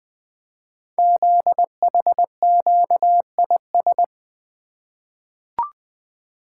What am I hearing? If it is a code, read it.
Morse code, heard as a keyed tone, about 20 wpm, 700 Hz. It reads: ZHQIS
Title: Code Groups - 5 Letters - 20wpm